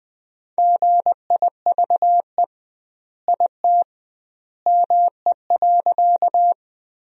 Morse code, heard as a keyed tone, about 21 wpm, 700 Hz. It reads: GIVE IT ME.
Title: 2x Word-Spacing - Sentences from Top 200 Words - 20wpm